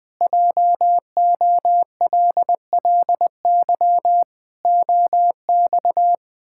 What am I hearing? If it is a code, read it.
Morse code, heard as a keyed tone, about 20 wpm, 700 Hz. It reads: JOLLY OX
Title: Pangram Challenge v2 - 20wpm